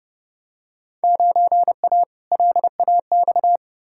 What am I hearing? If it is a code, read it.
Morse code, heard as a keyed tone, about 32 wpm, 700 Hz. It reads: 9A LAX
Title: Field Day Exchanges Encore - 30wpm